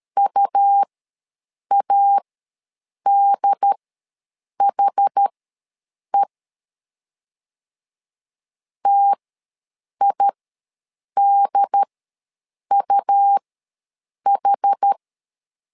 Morse code, heard as a keyed tone, about 13 wpm, spaced as for 7 wpm, 800 Hz. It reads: UADHE TIDUH
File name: LEZIONI / 2 UHDN / 2 UHD ws 12wpm 1722